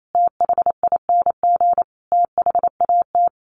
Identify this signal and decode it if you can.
Morse code, heard as a keyed tone, about 28 wpm, 700 Hz. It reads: THING THAT
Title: A Sound Of Thunder 28wpm